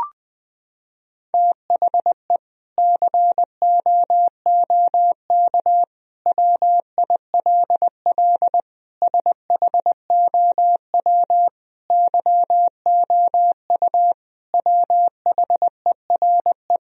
Morse code, heard as a keyed tone, about 20 wpm, 700 Hz. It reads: THE COOK WILL SHOW YOU WHERE
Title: Sentences from Top 1000 Words - Review Part 4 - 20wpm